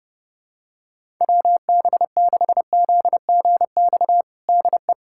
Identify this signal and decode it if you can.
Morse code, heard as a keyed tone, about 32 wpm, 700 Hz. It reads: WB6ZGX DE
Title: USA Callsign DE Callsign 30wpm